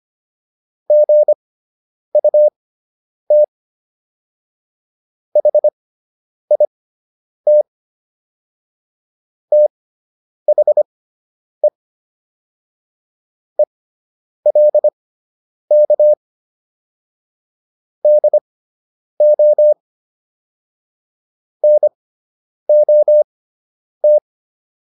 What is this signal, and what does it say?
Morse code, heard as a keyed tone, about 25 wpm, spaced as for 9 wpm, 600 Hz. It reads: GUT HIT THE ELK DO NOT